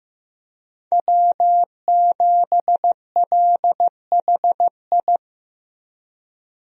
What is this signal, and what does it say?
Morse code, heard as a keyed tone, about 15 wpm, 700 Hz. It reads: W7LHI
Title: USA Call Signs - 5 Characters - Repeated 3x - 15wpm